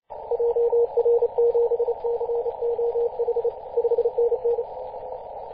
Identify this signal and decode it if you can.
Morse code, heard as a keyed tone, about 30 wpm, 470 Hz. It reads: JR7COH 5NN
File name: posts / 2016-mar-18-2347z-5j0p-21-025mhz-cw